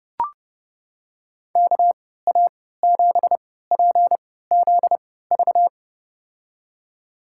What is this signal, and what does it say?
Morse code, heard as a keyed tone, about 32 wpm, spaced as for 17 wpm, 700 Hz. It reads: KA7PZV
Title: USA Call Signs - 6 Characters - 3x Character Spacing - 30wpm